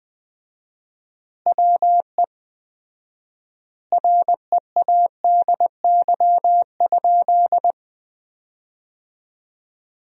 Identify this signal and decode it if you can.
Morse code, heard as a keyed tone, about 21 wpm, 700 Hz. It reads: WE READY?
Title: 4x Word-Spacing - Sentences from Top 400 Words - 20wpm